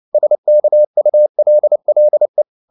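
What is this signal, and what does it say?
Morse code, heard as a keyed tone, about 29 wpm, 600 Hz. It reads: SKULLE
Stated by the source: Fast (QRQ) Morse at 29 wpm, from CW med Gnister.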